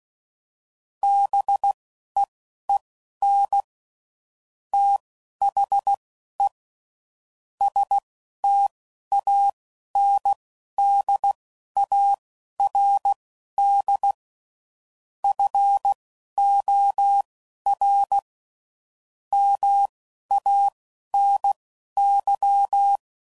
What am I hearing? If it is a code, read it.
Morse code, heard as a keyed tone, about 16 wpm, spaced as for 11 wpm, 800 Hz. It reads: BEEN THE STANDARD FOR MANY
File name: CW Code Groups @10WPM/Technical Subjects